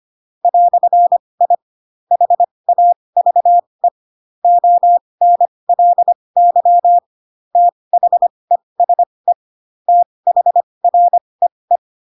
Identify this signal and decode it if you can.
Morse code, heard as a keyed tone, about 25 wpm, spaced as for 20 wpm, 700 Hz. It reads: "I HAVE ONLY THESE THREE
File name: Pinocchio / Pinocchio ch7 15 25 ik1hge